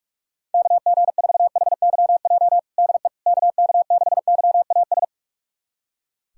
Morse code, heard as a keyed tone, about 44 wpm, 690 Hz. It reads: KG4HYJ DE KK6YAS
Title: USA Callsign DE Callsign - 6 Characters - 45wpm